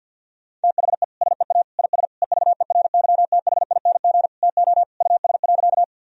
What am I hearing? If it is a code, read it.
Morse code, heard as a keyed tone, about 52 wpm, 700 Hz. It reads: THE SEA IS EVERYTHING TO US.
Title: Sentences from Top 1000 Words - Review Part 1 - 50wpm